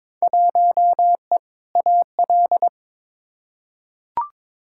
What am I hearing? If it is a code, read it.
Morse code, heard as a keyed tone, about 21 wpm, 700 Hz. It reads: 1E AL
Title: Field Day Exchanges - 22wpm